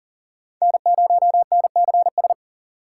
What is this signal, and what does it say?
Morse code, heard as a keyed tone, about 40 wpm, 700 Hz. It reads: N0NCS
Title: USA Call Signs - 5 Characters - 4x Character Spacing - 40wpm